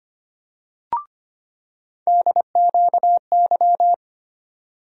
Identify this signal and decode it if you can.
Morse code, heard as a keyed tone, about 25 wpm, 700 Hz. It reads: DQY
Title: USA Call Sign Postfixes - 3 Letters 25wpm